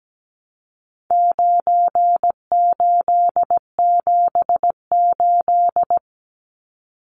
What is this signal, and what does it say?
Morse code, heard as a keyed tone, about 17 wpm, 700 Hz. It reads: 9878
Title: Numbers - 4-Digit 17wpm